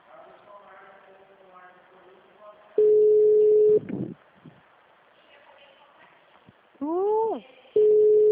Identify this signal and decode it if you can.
Ringback tone (Europe, 425 Hz), with speech.